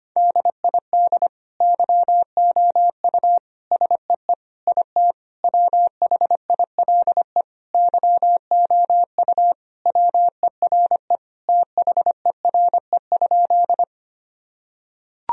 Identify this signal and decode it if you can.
Morse code, heard as a keyed tone, about 25 wpm, 700 Hz. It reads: DID YOU SEE IT WHILE YOU WERE THERE?
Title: Sentences from Top 1000 Words - Review Part 1 - 25wpm